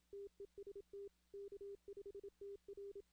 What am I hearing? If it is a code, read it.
Morse code, heard as a keyed tone, about 27 wpm, 400 Hz. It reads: TEST K5TR
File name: 2016 arrl dx cw / 2016-ARRL-DX-CW-K5TR- - 0-41